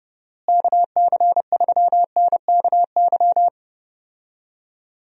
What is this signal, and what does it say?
Morse code, heard as a keyed tone, about 30 wpm, 700 Hz. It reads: KC3NKY
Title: USA Call Signs - 6 Characters - Repeated 3x - 30wpm